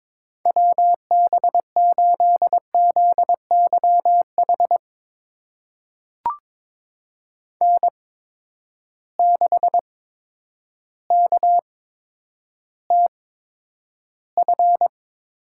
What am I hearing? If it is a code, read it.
Morse code, heard as a keyed tone, about 22 wpm, spaced as for 10 wpm, 700 Hz. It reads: WB8ZYH N6KTF
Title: USA Call Signs - 8x Character Spacing - 22wpm